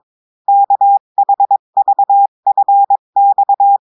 Morse code, heard as a keyed tone, about 22 wpm, 800 Hz. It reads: KHVFX
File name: cours-du-mercredi / audios / cours-853-du-2-mai-2024-etxte chiffre et texte clair-a-20-wpm